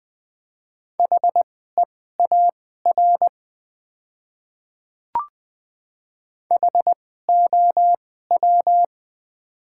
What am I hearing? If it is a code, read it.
Morse code, heard as a keyed tone, about 20 wpm, spaced as for 15 wpm, 700 Hz. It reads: HEAR HOW
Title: 2x Character Spacing - Top 300 Words - 20wpm